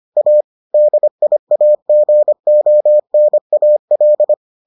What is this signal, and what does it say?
Morse code, heard as a keyed tone, about 25 wpm, 600 Hz. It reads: A DIAGONAL